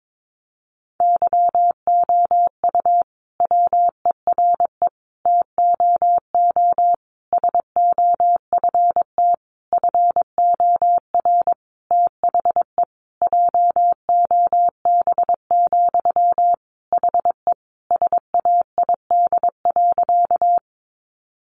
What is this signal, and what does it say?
Morse code, heard as a keyed tone, about 22 wpm, 700 Hz. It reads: YOU WERE TOO SOFT FOR THE JOB, HE SAID.
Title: Sentences from Top 1500 Words 22wpm